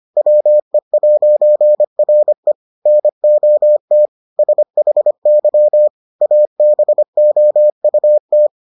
Morse code, heard as a keyed tone, about 25 wpm, 600 Hz. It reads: WE'RE NOT SHY ABOUT